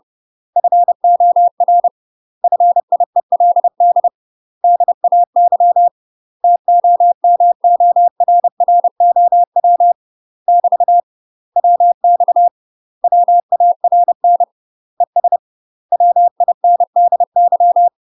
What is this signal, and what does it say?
Morse code, heard as a keyed tone, about 30 wpm, 700 Hz. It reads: FOR FIELD DAY TOMORROW <BT> WX WARN ES WINDY
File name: UK MASTER QSOs TRAINING / 30WPM / UK MASTER QSOs 0011